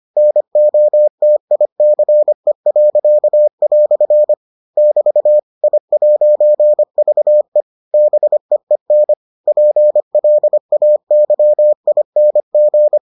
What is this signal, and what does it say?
Morse code, heard as a keyed tone, about 25 wpm, 600 Hz. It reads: NOTICE." <BT> I'VE BEEN PLAYING